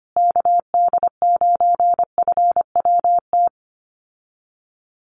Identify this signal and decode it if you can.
Morse code, heard as a keyed tone, about 25 wpm, 700 Hz. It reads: KD9FWT
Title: USA Call Signs Repeated 2x 25wpm